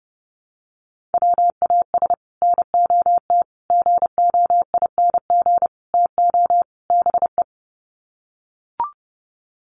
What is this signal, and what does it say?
Morse code, heard as a keyed tone, about 28 wpm, 700 Hz. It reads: WAS NOT GOING TO BE
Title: Sets of 5 Words Encore 30wpm